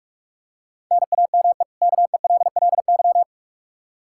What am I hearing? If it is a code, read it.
Morse code, heard as a keyed tone, about 46 wpm, 700 Hz. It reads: NAME KELLY
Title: K1USN SST QSO Practice 45wpm